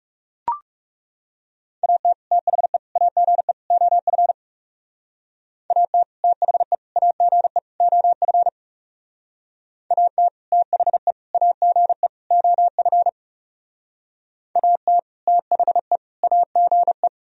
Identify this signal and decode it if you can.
Morse code, heard as a keyed tone, about 36 wpm, 700 Hz. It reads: AT THE AGE OF AT THE AGE OF AT THE AGE OF AT THE AGE
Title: Speed-Racing - Sets of 4 Words 30wpm